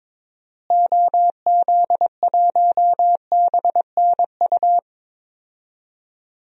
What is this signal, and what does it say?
Morse code, heard as a keyed tone, about 22 wpm, 700 Hz. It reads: OZ1BNU
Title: DXCC-100 Call Signs - 6 Characters - Repeated 2x - 22wpm